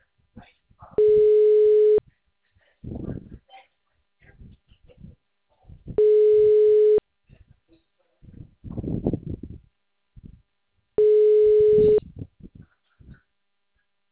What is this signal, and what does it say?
Ringback tone (Europe, 425 Hz), with speech.
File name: out- -2014- - - .55867